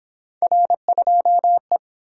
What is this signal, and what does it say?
Morse code, heard as a keyed tone, about 29 wpm, 700 Hz. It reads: R2E